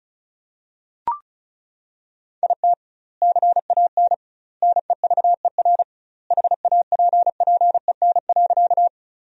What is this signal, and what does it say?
Morse code, heard as a keyed tone, about 34 wpm, 700 Hz. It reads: IT CAN NEVER HAPPEN.
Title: 2x Word-Spacing - Sentences from Top 400 Words - 35wpm